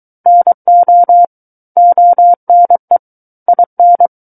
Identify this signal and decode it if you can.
Morse code, heard as a keyed tone, about 23 wpm, 700 Hz. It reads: NO ONE IN
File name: Book One / 14-Chapter 14 - 23 WPM